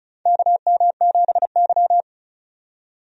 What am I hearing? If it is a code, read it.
Morse code, heard as a keyed tone, about 36 wpm, 700 Hz. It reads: KM7Y
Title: USA Call Signs - 4 Characters - 35wpm